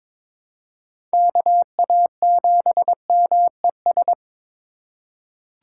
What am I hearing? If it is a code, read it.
Morse code, heard as a keyed tone, about 22 wpm, 700 Hz. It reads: KA7MES
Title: USA Call Signs - 6 Characters - 8x Character Spacing - 22wpm